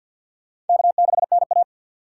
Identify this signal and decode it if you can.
Morse code, heard as a keyed tone, about 53 wpm, 700 Hz. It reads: K6NA
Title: K1USN SST QSO Practice - 50wpm